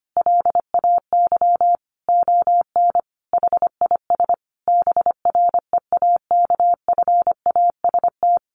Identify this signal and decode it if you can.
Morse code, heard as a keyed tone, about 25 wpm, 700 Hz. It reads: LAY ON HIS BREAKFAST
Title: Sentences from Top 1500 Words 25wpm